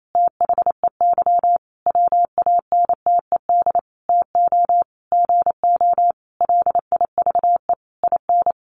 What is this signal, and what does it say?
Morse code, heard as a keyed tone, about 28 wpm, 700 Hz. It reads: THEY WANTED TO GO LIVE IN
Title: A Sound Of Thunder 28wpm